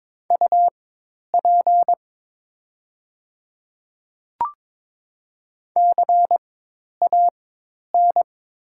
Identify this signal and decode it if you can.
Morse code, heard as a keyed tone, about 22 wpm, spaced as for 10 wpm, 700 Hz. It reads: UP CAN
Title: 4x Character Spacing - Top 100 Words - 22wpm